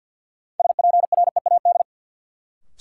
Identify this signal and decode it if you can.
Morse code, heard as a keyed tone, about 52 wpm, 700 Hz. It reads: SPREAD
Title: Sentences from Top 1200 Words - New Word Drill - 50wpm